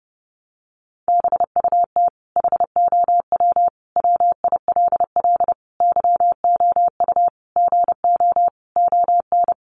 Morse code, heard as a keyed tone, about 30 wpm, 700 Hz. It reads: BUT HOW WILL YOU GO ON
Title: Sentences from Top 1100 Words 30wpm